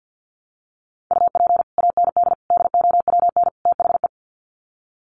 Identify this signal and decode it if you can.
Morse code, heard as a keyed tone, about 49 wpm, 700 Hz. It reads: UP AND DOWN THE
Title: Sets of 4 Words 50wpm